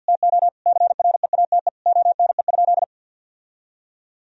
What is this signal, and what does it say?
Morse code, heard as a keyed tone, about 50 wpm, 700 Hz. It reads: TO CREATE ONE?
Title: Sentences from Top 1000 Words - Review Part 3 - 50wpm